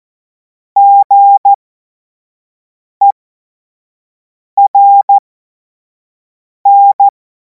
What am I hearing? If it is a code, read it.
Morse code, heard as a keyed tone, about 14 wpm, spaced as for 5 wpm, 800 Hz. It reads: GERN
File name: Morsekurs / Witze / Witz 23. Juni 2026 25 F70